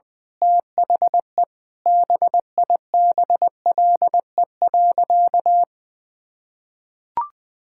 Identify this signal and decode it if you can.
Morse code, heard as a keyed tone, about 20 wpm, 700 Hz. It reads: THE BIBLE.
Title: Sentences with 9-Word Phrases - 20wpm